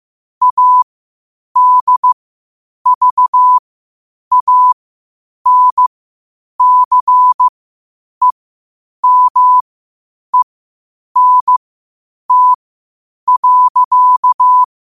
Morse code, heard as a keyed tone, about 15 wpm, spaced as for 8 wpm, 1000 Hz. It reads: ADVANCEMENT.
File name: qotdfiles / 8wpm